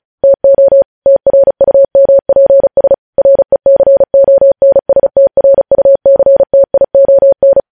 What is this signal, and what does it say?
Morse code, heard as a keyed tone, about 36 wpm, 550 Hz. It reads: TO TRUMPS RECONSTRUCTION
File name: podcast / News headlines-1x-35x35